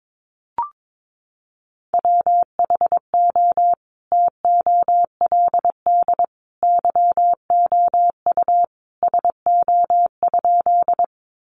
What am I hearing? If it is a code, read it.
Morse code, heard as a keyed tone, about 22 wpm, 700 Hz. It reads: WHO TOLD YOU SO?
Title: Sentences from Top 400 Words 22wpm